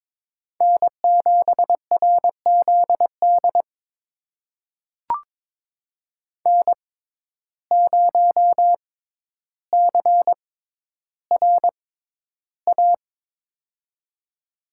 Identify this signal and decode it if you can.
Morse code, heard as a keyed tone, about 22 wpm, spaced as for 11 wpm, 700 Hz. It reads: N7RZD N0CRA
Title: USA Call Signs - 5 Characters - 6x Character Spacing - 22wpm